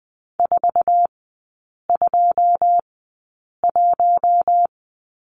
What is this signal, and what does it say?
Morse code, heard as a keyed tone, about 20 wpm, spaced as for 8 wpm, 690 Hz. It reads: 421
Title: 093 Lesson 16e - Letter c - Three Characters 20wpm